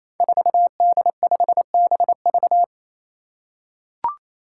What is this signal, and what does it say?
Morse code, heard as a keyed tone, about 28 wpm, 700 Hz. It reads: 4D5BV
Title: Five Challenging Letter-Number Combinations 28wpm